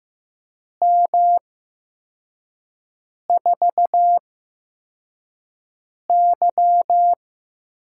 Morse code, heard as a keyed tone, about 15 wpm, spaced as for 4 wpm, 700 Hz. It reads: M4Y
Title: USA Call Signs - 4 Characters - 8x Character Spacing - 15wpm